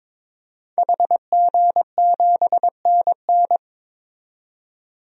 Morse code, heard as a keyed tone, about 21 wpm, 700 Hz. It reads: HG7NN
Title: DXCC-100 Call Signs - 5 Characters - 22wpm